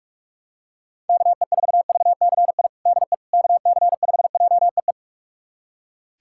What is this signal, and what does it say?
Morse code, heard as a keyed tone, about 46 wpm, 700 Hz. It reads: KE4VCI DE KC5JEE
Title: USA Callsign DE Callsign - 6 Characters - 45wpm